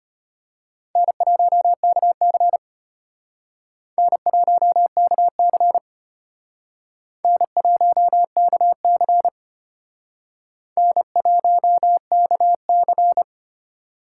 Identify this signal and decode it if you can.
Morse code, heard as a keyed tone, about 32 wpm, 700 Hz. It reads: N1KC N1KC N1KC N1KC